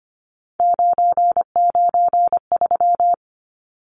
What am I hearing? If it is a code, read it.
Morse code, heard as a keyed tone, about 25 wpm, 700 Hz. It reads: 993
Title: Numbers - 3-Digit 25wpm